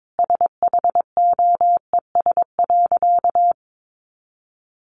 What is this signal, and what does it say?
Morse code, heard as a keyed tone, about 22 wpm, 700 Hz. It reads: SHOES.
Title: A Sound Of Thunder 22wpm